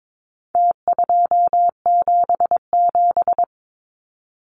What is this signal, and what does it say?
Morse code, heard as a keyed tone, about 22 wpm, 700 Hz. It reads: T277
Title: Numbers - Contest Style - 4-Digit 22wpm